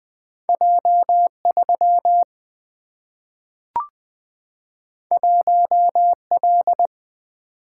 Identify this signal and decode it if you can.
Morse code, heard as a keyed tone, about 20 wpm, 700 Hz. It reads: J3 1L
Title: Single-Letter Single-Number 20wpm